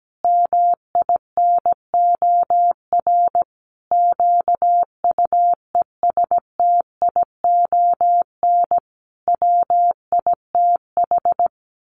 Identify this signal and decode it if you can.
Morse code, heard as a keyed tone, about 17 wpm, 700 Hz. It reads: MINOR QUESTION WITH
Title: Sentences from Top 1500 Words 17wpm